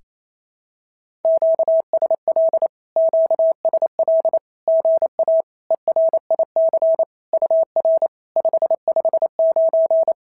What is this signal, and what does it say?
Morse code, heard as a keyed tone, about 28 wpm, 650 Hz. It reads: QSL QSL GA ERIC UR 559
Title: SOTA QSO Practice 28wpm